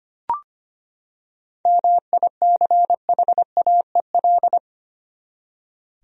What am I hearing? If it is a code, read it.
Morse code, heard as a keyed tone, about 25 wpm, 700 Hz. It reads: MICHAEL
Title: USA Common Names - 25wpm